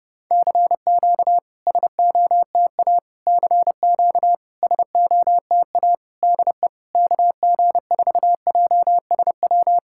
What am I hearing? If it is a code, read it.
Morse code, heard as a keyed tone, about 32 wpm, 700 Hz. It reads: CQ SOTA CQ SOTA DE KG4JSW
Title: SOTA QSO Practice 30wpm